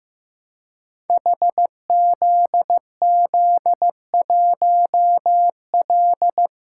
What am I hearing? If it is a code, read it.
Morse code, heard as a keyed tone, about 16 wpm, 700 Hz. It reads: HZZ1L
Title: Code Groups - 5 Letter:Numbers - 15wpm